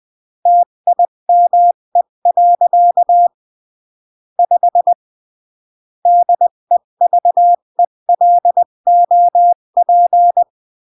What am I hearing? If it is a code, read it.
Morse code, heard as a keyed tone, about 20 wpm, 700 Hz. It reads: TIME. 5 DEVELOP
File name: CW / jodorowsky / 82 commandments 20-15